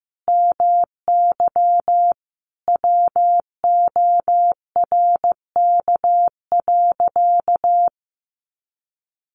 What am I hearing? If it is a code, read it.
Morse code, heard as a keyed tone, about 15 wpm, 700 Hz. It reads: MY WORK.
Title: Sentences from Top 300 Words 15wpm